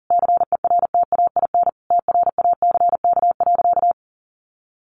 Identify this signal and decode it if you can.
Morse code, heard as a keyed tone, about 40 wpm, 700 Hz. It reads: CERTAIN TRACK.
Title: Sentences from Top 1100 Words 40wpm